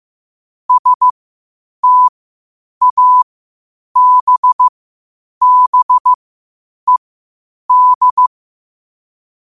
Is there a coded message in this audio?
Yes: Morse code from 0.7 to 8.3 s, heard as a keyed tone, about 15 wpm, spaced as for 8 wpm, 1000 Hz. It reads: STABBED